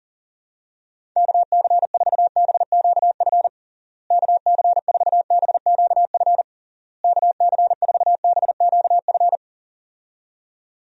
Morse code, heard as a keyed tone, about 40 wpm, 700 Hz. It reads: KC4BQF KC4BQF KC4BQF
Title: USA Call Signs - 6 Characters - Repeated 3x - 40wpm